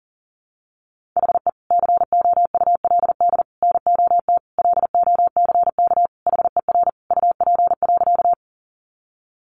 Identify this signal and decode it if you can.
Morse code, heard as a keyed tone, about 40 wpm, 700 Hz. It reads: HE COULD NOT LOCK HER UP.
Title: Sentences from Top 1300 Words 40wpm